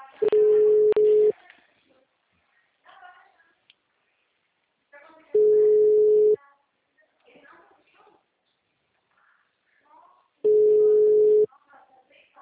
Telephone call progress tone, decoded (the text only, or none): ringback tone